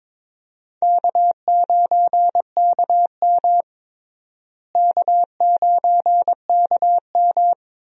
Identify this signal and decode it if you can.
Morse code, heard as a keyed tone, about 22 wpm, 700 Hz. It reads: K9KM K9KM
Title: USA Call Signs - 4 Characters - Repeated 3x - 22wpm